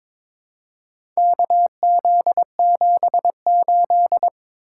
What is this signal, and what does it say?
Morse code, heard as a keyed tone, about 22 wpm, 700 Hz. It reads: KZ78
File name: practice / 22wpm / Four Letter-Number Combinations - 22wpm